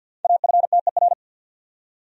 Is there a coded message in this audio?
Yes: Morse code from 0.2 to 1.1 s, heard as a keyed tone, about 63 wpm, 700 Hz. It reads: AFTER